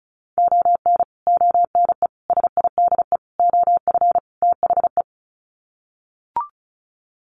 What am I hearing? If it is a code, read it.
Morse code, heard as a keyed tone, about 34 wpm, 700 Hz. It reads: ON ONE SIDE OF THE
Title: Sets of 5 Words 35wpm